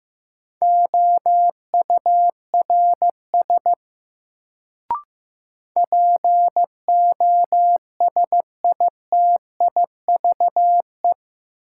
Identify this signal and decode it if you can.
Morse code, heard as a keyed tone, about 15 wpm, 700 Hz. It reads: OURS POSITIVE
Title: Sentences from Top 1000 Words - Review Part 5 - New Word Drill - 15wpm